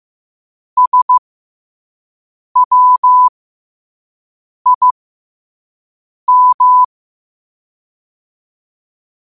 Morse code, heard as a keyed tone, about 15 wpm, spaced as for 5 wpm, 1000 Hz. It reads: SWIM